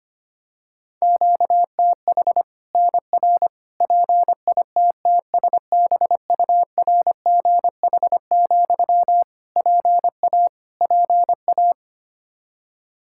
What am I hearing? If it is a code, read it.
Morse code, heard as a keyed tone, about 25 wpm, 700 Hz. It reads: QTH NR PITTSBURGH, PA PA
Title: Practice QSO - Exchange 1 - Beginning - 25wpm